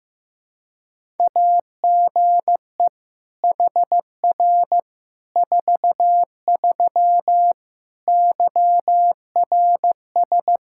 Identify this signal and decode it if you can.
Morse code, heard as a keyed tone, about 16 wpm, 700 Hz. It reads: AGE HR 43 YRS
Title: K1USN SST QSO Practice 15wpm